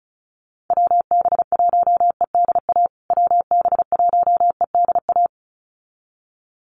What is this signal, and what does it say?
Morse code, heard as a keyed tone, about 34 wpm, 700 Hz. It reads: WB1EDA WB1EDA
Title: USA Call Signs Repeated 2x 35wpm